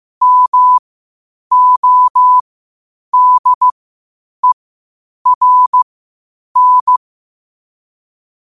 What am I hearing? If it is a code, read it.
Morse code, heard as a keyed tone, about 15 wpm, spaced as for 8 wpm, 1000 Hz. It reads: MODERN
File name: qotdfiles / 8wpm